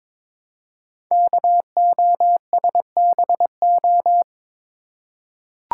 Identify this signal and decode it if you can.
Morse code, heard as a keyed tone, about 23 wpm, 700 Hz. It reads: KOSBO
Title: Code Groups - 5 Letters - 22wpm